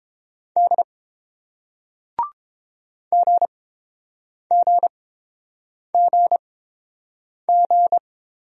Morse code, heard as a keyed tone, about 31 wpm, spaced as for 7 wpm, 700 Hz. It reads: D GGGG